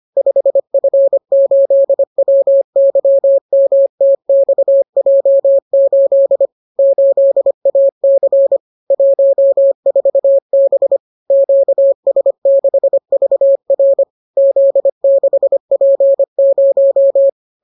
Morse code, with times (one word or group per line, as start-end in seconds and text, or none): 0.2-6.5 5F8WYMTXJ8
6.8-8.6 8AC
8.9-11.0 14B
11.3-14.0 QS6VR
14.4-17.3 Z6P0